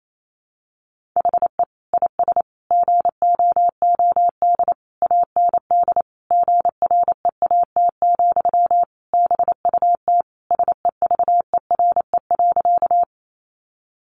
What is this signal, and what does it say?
Morse code, heard as a keyed tone, about 28 wpm, 700 Hz. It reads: HE IS GOOD AND GREAT, BUT SEVERE.
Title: Sentences from Top 1400 Words 28wpm